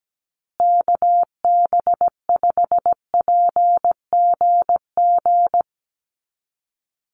Morse code, heard as a keyed tone, about 17 wpm, 700 Hz. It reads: KB5PGG
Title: USA Call Signs Repeated 2x 17wpm